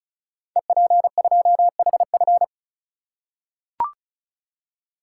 Morse code, heard as a keyed tone, about 36 wpm, 700 Hz. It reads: EP2HF
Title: Code Groups - 5 Letter:Numbers - 35wpm